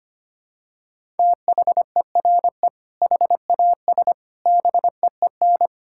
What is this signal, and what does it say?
Morse code, heard as a keyed tone, about 25 wpm, 700 Hz. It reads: THERE HAS BEEN
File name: practice / 25wpm / Sentences with 4-Word Phrases - 25wpm